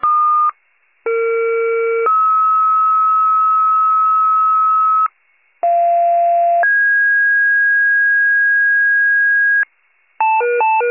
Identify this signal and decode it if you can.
Two-tone page, heard as a one-second tone, then a three-second tone, about 460 then 1290 Hz, about 690 then 1650 Hz, with noise. It is heard from 1.1 to 9.6 s.
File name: wcalert